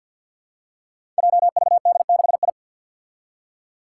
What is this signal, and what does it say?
Morse code, heard as a keyed tone, about 50 wpm, 700 Hz. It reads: JVD6I
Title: Five Challenging Letter-Number Combinations 50wpm